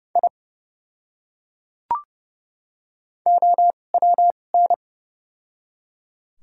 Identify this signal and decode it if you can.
Morse code, heard as a keyed tone, about 30 wpm, spaced as for 22 wpm, 700 Hz. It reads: I OWN
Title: 2x Character Spacing - Top 500 Words - 30wpm